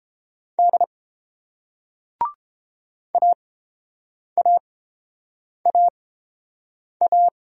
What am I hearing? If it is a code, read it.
Morse code, heard as a keyed tone, about 31 wpm, spaced as for 7 wpm, 700 Hz. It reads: D AAAA